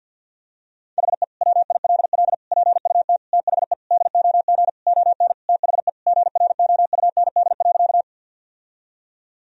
Morse code, heard as a keyed tone, about 51 wpm, 700 Hz. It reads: HE WILL PUT THE DOG ON THE GROUND.
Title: Sentences from Top 1000 Words - Review Part 2 - 50wpm